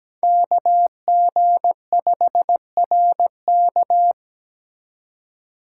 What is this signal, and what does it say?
Morse code, heard as a keyed tone, about 17 wpm, 700 Hz. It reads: KG5RK
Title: USA Call Signs - 8x Character Spacing - 17wpm